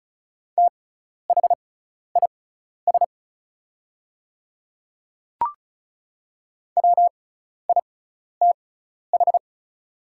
Morse code, heard as a keyed tone, about 36 wpm, spaced as for 12 wpm, 700 Hz. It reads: THIS WITH
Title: 6x Character Spacing - Top 100 Words - 35wpm